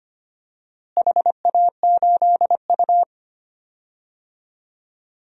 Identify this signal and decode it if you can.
Morse code, heard as a keyed tone, about 25 wpm, 700 Hz. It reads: HA8U
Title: DXCC-100 Call Signs - 4 Characters - Repeated 2x - 25wpm